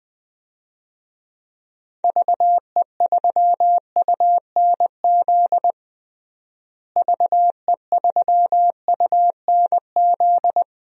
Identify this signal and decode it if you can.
Morse code, heard as a keyed tone, about 20 wpm, 700 Hz. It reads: VE3UNZ VE3UNZ
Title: DXCC-100 Call Signs - 6 Characters - Repeated 2x - 20wpm